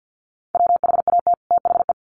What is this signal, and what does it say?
Morse code, heard as a keyed tone, about 50 wpm, 700 Hz. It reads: WHAT THE
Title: Sets of 2 Words 50wpm